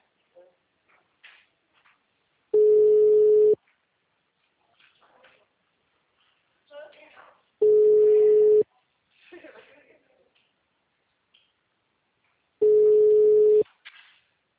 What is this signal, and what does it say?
Ringback tone (Europe, 425 Hz), with speech.